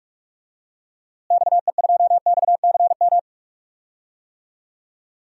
Morse code, heard as a keyed tone, about 46 wpm, 700 Hz. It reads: XE2XCM
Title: DXCC-100 Call Signs - 6 Characters - Repeated 3x - 45wpm